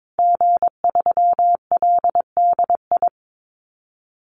Morse code, heard as a keyed tone, about 22 wpm, 700 Hz. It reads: G3LDI
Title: Call Signs Repeated 2x - 22wpm